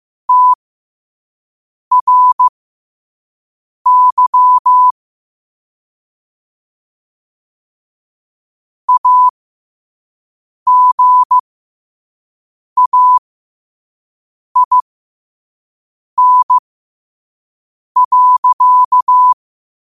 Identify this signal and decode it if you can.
Morse code, heard as a keyed tone, about 15 wpm, spaced as for 5 wpm, 1000 Hz. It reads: TRY AGAIN.
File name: qotdfiles / 5wpm